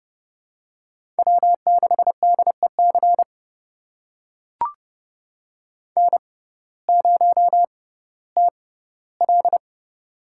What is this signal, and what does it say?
Morse code, heard as a keyed tone, about 30 wpm, spaced as for 17 wpm, 700 Hz. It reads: W6DEC N0TL
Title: USA Call Signs - 6x Character Spacing - 30wpm